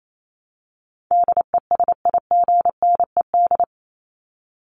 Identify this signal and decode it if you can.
Morse code, heard as a keyed tone, about 28 wpm, 700 Hz. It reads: DESIGNED
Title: ED Suffix - Top 100 Words with Base Words 28wpm